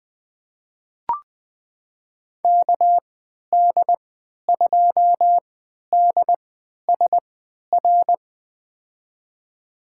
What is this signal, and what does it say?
Morse code, heard as a keyed tone, about 20 wpm, spaced as for 11 wpm, 700 Hz. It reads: KD2DSR
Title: USA Call Signs - 6 Characters - 3x Character Spacing - 20wpm